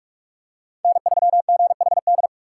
Morse code, heard as a keyed tone, about 44 wpm, 700 Hz. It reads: N3GHD
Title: USA Call Signs - 5 Characters - 4x Character Spacing - 45wpm